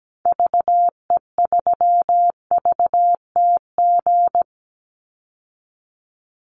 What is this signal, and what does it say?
Morse code, heard as a keyed tone, about 17 wpm, 700 Hz. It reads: VE3VTG
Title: Call Signs - 17wpm